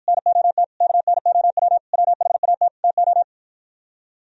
Morse code, heard as a keyed tone, about 52 wpm, 700 Hz. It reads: NOT KNOW WHAT TO